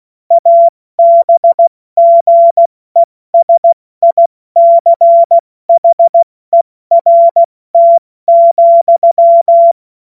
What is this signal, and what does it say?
Morse code, heard as a keyed tone, about 16 wpm, spaced as for 14 wpm, 680 Hz. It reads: ABGESICHERT,
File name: Morsekurs / Witze / Witz 02. Mai 2023 70 F80